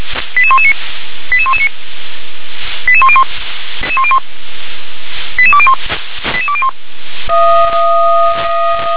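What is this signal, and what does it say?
Five-tone selective call (ZVEI), heard as quick runs of five tones, tones of about 70 ms. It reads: 88119 88119 88101 88201 88201
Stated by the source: SDR-received ZVEI five-tone selective call, address 88126.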